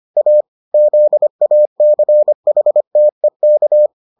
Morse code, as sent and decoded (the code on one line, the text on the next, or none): .- / --.. .- -.-. .... - . -.-
A ZACHTEK